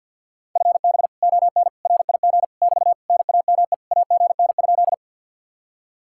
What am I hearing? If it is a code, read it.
Morse code, heard as a keyed tone, about 61 wpm, 700 Hz. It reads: FB ON RIG <BT> NAME AGN?
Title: Speed-Racing Common QSO Phrases - 50wpm